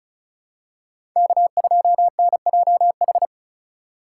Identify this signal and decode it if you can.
Morse code, heard as a keyed tone, about 34 wpm, 700 Hz. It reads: K2NJH
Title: USA Call Signs - 2x Character Spacing - 35wpm